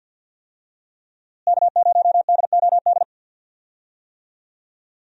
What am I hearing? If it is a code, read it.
Morse code, heard as a keyed tone, about 49 wpm, 700 Hz. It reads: K0DOD
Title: USA Call Signs - 5 Characters - Repeated 3x - 50wpm